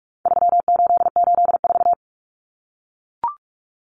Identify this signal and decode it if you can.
Morse code, heard as a keyed tone, about 44 wpm, 700 Hz. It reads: 3884
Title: Common Cut Numbers - 4-Digit 45wpm